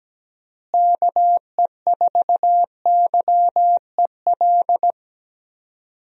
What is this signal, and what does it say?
Morse code, heard as a keyed tone, about 17 wpm, 700 Hz. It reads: KE4YEL
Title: USA Call Signs - 6 Characters - Repeated 3x - 17wpm